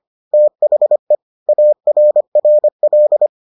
Morse code, heard as a keyed tone, about 25 wpm, 600 Hz. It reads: THE ARRL